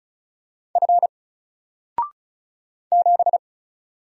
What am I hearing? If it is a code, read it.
Morse code, heard as a keyed tone, about 36 wpm, 700 Hz. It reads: F 7
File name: Single-Letter-Number-35wpm